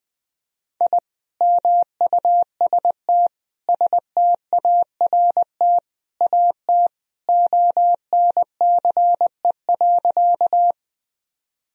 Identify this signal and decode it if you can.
Morse code, heard as a keyed tone, about 20 wpm, 700 Hz. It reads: I MUST START AT ONCE.
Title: Sentences from Top 1000 Words - Review Part 1 - 20wpm